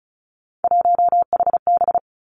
Morse code, heard as a keyed tone, about 36 wpm, 700 Hz. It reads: 1HB
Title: Three Challenging Letter-Number Combinations 35wpm